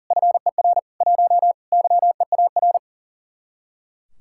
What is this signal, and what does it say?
Morse code, heard as a keyed tone, about 47 wpm, 700 Hz. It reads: FER 1 YEAR
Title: Speed-Racing Common QSO Phrases Encore - 40wpm